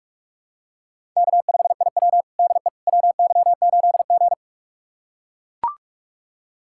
Morse code, heard as a keyed tone, about 41 wpm, 700 Hz. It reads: K5IW DE WY8G
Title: USA Callsign DE Callsign - 4 Characters - 45wpm